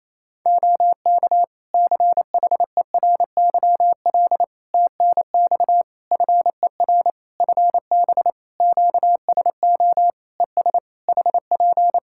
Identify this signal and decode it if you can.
Morse code, heard as a keyed tone, about 27 wpm, 700 Hz. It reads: OK CHERYL TNX FER FB QSO ES HP
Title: Practice QSO - Exchange 4 - Ending - 28wpm